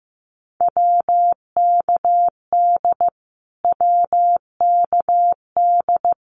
Morse code, heard as a keyed tone, about 15 wpm, 700 Hz. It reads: WKD WKD
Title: Abbreviations Repeated 3x - 15wpm